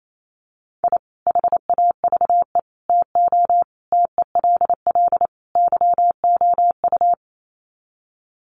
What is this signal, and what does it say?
Morse code, heard as a keyed tone, about 28 wpm, 700 Hz. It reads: I HAVE TO TELL YOU
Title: Sets of 5 Words 28wpm